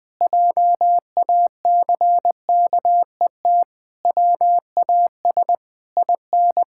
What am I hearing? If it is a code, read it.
Morse code, heard as a keyed tone, about 20 wpm, 700 Hz. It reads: JACKET WAS IN
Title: Pangram Challenge v2 - 20wpm